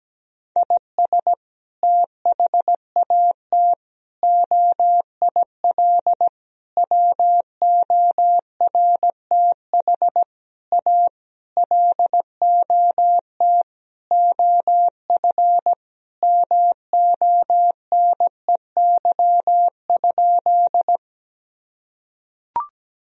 Morse code, heard as a keyed tone, about 17 wpm, 700 Hz. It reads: IS THAT OIL WORTH A LOT OF MONEY?
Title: Sentences from Top 1000 Words - Review Part 4 - 17wpm